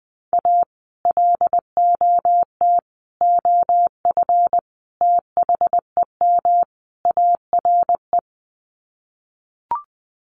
Morse code, heard as a keyed tone, about 20 wpm, 700 Hz. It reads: A LOT OF THEM ARE
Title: Sets of 5 Words Encore2 20wpm